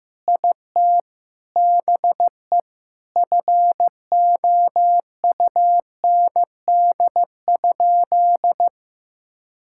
Morse code, heard as a keyed tone, about 16 wpm, 700 Hz. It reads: IT BE FOUND?
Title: Sentences from Top 1600 Words 15wpm